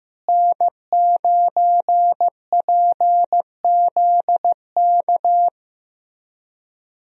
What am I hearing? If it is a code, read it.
Morse code, heard as a keyed tone, about 16 wpm, 700 Hz. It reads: N9PZK
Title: USA Call Signs - 5 Characters - Repeated 3x - 15wpm